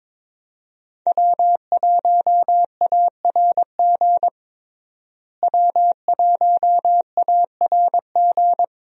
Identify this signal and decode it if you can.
Morse code, heard as a keyed tone, about 22 wpm, 700 Hz. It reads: W1ARG W1ARG
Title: USA Call Signs - 5 Characters - Repeated 2x - 22wpm